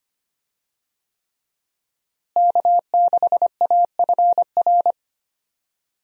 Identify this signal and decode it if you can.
Morse code, heard as a keyed tone, about 25 wpm, 700 Hz. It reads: K6AFR
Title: USA Call Signs - 5 Characters - Repeated 3x - 25wpm